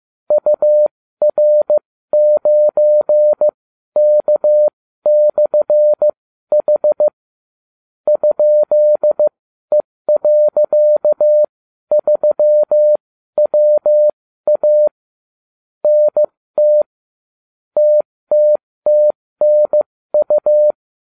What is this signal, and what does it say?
Morse code, heard as a keyed tone, about 15 wpm, spaced as for 12 wpm, 600 Hz. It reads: UR9K/H ?E.3WA NT TTTNU